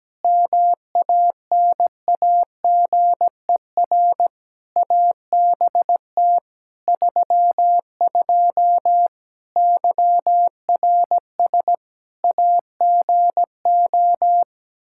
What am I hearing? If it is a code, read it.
Morse code, heard as a keyed tone, about 17 wpm, 700 Hz. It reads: MANAGER ABT 32 YRS AGO
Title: Practice QSO - Exchange 3 - More Info - 17wpm